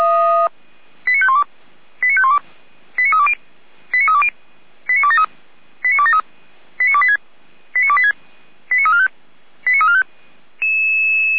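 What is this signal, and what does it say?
Five-tone selective call (ZVEI), heard as quick runs of five tones, tones of about 70 ms. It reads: 89512 89512 89320 89320 89273 89273 89286 89286 89345 89345
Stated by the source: Received on an SDR as a ZVEI five-tone call.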